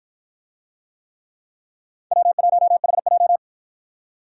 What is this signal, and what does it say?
Morse code, heard as a keyed tone, about 48 wpm, 700 Hz. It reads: W1HJ